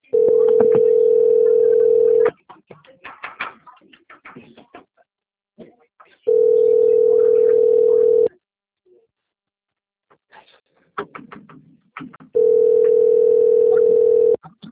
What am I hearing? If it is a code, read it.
Ringback tone (North America), 440 + 480 Hz.
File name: out- - - - - .45034